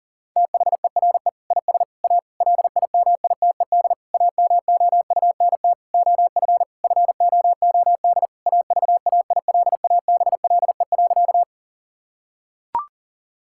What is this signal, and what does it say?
Morse code, heard as a keyed tone, about 40 wpm, 700 Hz. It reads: THERE IS A LIMITED AMOUNT OF FOOD AVAILABLE.
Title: Sentences from Top 1000 Words - Review Part 6 - 40wpm